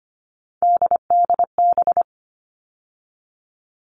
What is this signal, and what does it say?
Morse code, heard as a keyed tone, about 25 wpm, 700 Hz. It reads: DDB
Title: All Cut Numbers - 3-Digit 25wpm